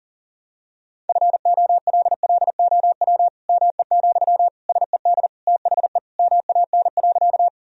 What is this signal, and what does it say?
Morse code, heard as a keyed tone, about 40 wpm, 700 Hz. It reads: FOLLOW ME, SED THE MAN.
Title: Sentences from Top 200 Words and Abbreviations 40wpm